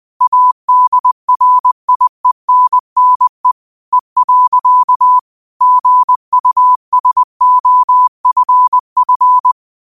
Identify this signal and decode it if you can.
Morse code, heard as a keyed tone, about 20 wpm, 1000 Hz. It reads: ADRIENNE E. GUSOFF
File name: qotdfiles / 20wpm